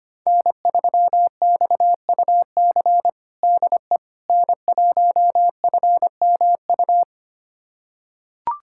Morse code, heard as a keyed tone, about 25 wpm, 700 Hz. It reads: N3XUC DE N1FMU
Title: USA Callsign DE Callsign - 5 Characters - 25wpm